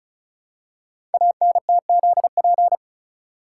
Morse code, heard as a keyed tone, about 36 wpm, 700 Hz. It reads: ANTZP
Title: Code Groups - 5 Letters - 35wpm